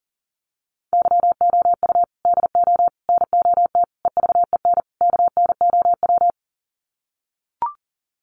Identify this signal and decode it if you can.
Morse code, heard as a keyed tone, about 39 wpm, 700 Hz. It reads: YOU DO NOT EVEN KNOW
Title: Sets of 5 Words Encore2 40wpm